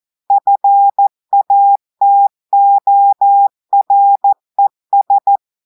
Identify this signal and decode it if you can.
Morse code, heard as a keyed tone, about 14 wpm, 800 Hz. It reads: FATORES
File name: audios / portugais / portugais texte en clair 2 vitesse 14 mots minutes